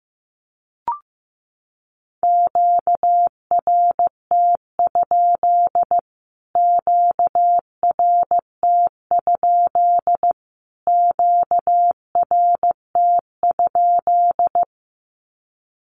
Morse code, heard as a keyed tone, about 15 wpm, 700 Hz. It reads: QRT? QRT? QRT?
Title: Q-Codes Repeated 3x 15wpm